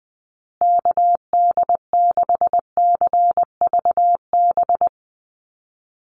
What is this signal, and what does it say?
Morse code, heard as a keyed tone, about 20 wpm, 700 Hz. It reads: KD6CVB
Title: USA Call Signs Repeated 3x 20wpm